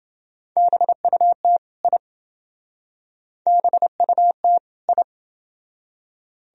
Morse code, heard as a keyed tone, about 28 wpm, 700 Hz. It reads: BUT I BUT I